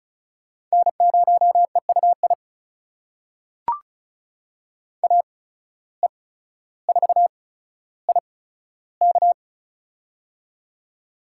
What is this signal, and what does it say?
Morse code, heard as a keyed tone, about 36 wpm, spaced as for 15 wpm, 700 Hz. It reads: N0EUI AE4IK
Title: USA Call Signs - 5 Characters - 8x Character Spacing - 35wpm